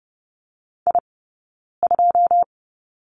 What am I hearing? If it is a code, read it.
Morse code, heard as a keyed tone, about 30 wpm, spaced as for 9 wpm, 690 Hz. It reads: I2
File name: Lesson 17d - Letter u - Two Characters 30wpm